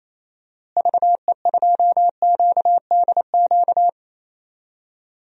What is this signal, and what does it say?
Morse code, heard as a keyed tone, about 28 wpm, 700 Hz. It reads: VE2QDQ
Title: DXCC-100 Call Signs - 6 Characters - Repeated 2x - 28wpm